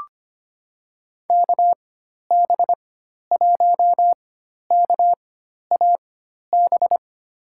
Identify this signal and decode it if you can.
Morse code, heard as a keyed tone, about 25 wpm, spaced as for 12 wpm, 700 Hz. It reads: KB1KAB
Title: USA Call Signs - 6 Characters - 4x Character Spacing - 25wpm